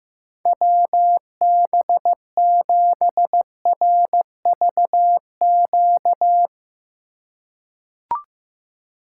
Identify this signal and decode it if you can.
Morse code, heard as a keyed tone, about 15 wpm, 700 Hz. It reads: WB7RVQ
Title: USA Call Signs - 6 Characters - 3x Character Spacing - 15wpm